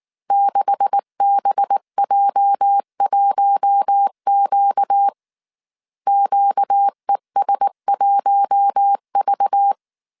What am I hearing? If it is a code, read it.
Morse code, heard as a keyed tone, about 19 wpm, 800 Hz. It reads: 6BJ1Q QES1V
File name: CW / DET / 1 9 1500C S A 18wpm 2459